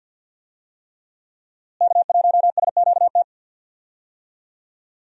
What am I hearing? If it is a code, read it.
Morse code, heard as a keyed tone, about 52 wpm, 700 Hz. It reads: K1SQT
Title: USA Call Signs - 5 Characters - Repeated 3x - 50wpm